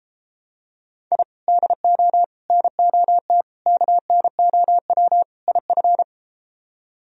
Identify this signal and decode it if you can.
Morse code, heard as a keyed tone, about 32 wpm, 700 Hz. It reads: I DO NOT KNOW IF